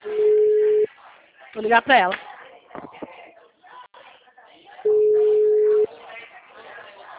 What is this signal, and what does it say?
Ringback tone (Europe, 425 Hz), with speech.